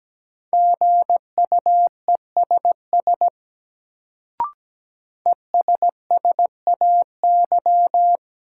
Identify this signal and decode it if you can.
Morse code, heard as a keyed tone, about 17 wpm, 700 Hz. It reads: GUESS ESSAY
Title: Sentences from Top 1500 Words - New Word Drill Alt - 17wpm